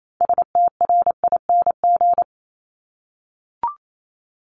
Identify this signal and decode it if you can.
Morse code, heard as a keyed tone, about 28 wpm, 700 Hz. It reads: STRING
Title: Multi-letter Phonemes Repeated 3x 28wpm